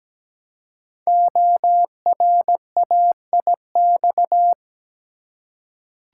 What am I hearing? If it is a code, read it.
Morse code, heard as a keyed tone, about 17 wpm, 700 Hz. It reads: ORAIX
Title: Code Groups - 5 Letters - 17wpm